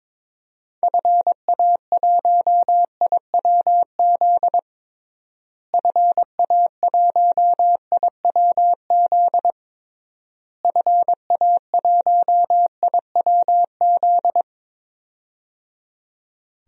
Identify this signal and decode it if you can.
Morse code, heard as a keyed tone, about 22 wpm, 700 Hz. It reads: FA1IWZ FA1IWZ FA1IWZ
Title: DXCC-100 Call Signs - 6 Characters - Repeated 3x - 22wpm